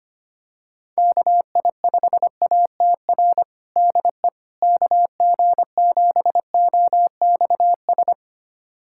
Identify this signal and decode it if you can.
Morse code, heard as a keyed tone, about 25 wpm, 700 Hz. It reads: KI5ATR DE KG7OXS
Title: USA Callsign DE Callsign 25wpm